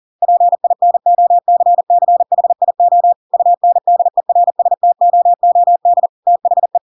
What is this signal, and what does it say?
Morse code, heard as a keyed tone, about 40 wpm, 700 Hz. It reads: PINOCCHIO UNDERSTOOD THE
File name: Pinocchio / Pinocchio ch8 40 40 ik1hge